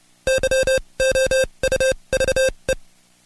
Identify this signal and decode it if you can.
Morse code, heard as a keyed tone, about 29 wpm, spaced as for 23 wpm, 530 Hz. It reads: YOUVE
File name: Code Quick / AudioDownld2 / Seg138-Stepper11 23WPM